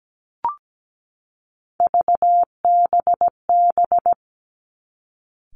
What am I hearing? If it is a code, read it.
Morse code, heard as a keyed tone, about 17 wpm, 700 Hz. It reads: VBB
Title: Three Challenging Letter-Number Combinations 17wpm